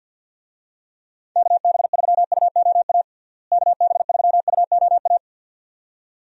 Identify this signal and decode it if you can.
Morse code, heard as a keyed tone, about 53 wpm, 700 Hz. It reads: KB3UOA KB3UOA
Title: USA Call Signs - 6 Characters - Repeated 2x - 50wpm